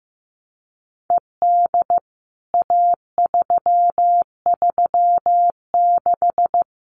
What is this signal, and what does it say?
Morse code, heard as a keyed tone, about 15 wpm, 700 Hz. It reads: ED A336
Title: CWT Exchange 15wpm